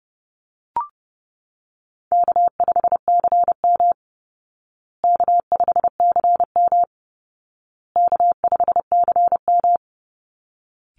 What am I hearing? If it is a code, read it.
Morse code, heard as a keyed tone, about 30 wpm, 700 Hz. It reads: K5CM K5CM K5CM
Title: Call Signs Repeated 2x - 30wpm